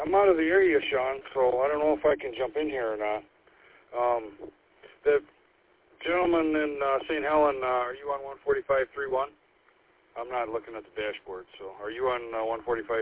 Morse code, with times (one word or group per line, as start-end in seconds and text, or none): none